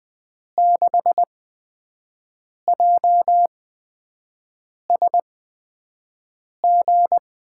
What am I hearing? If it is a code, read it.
Morse code, heard as a keyed tone, about 20 wpm, 700 Hz. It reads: 6 J S G
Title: USA Call Signs - 5 Characters - 8x Character Spacing - 20wpm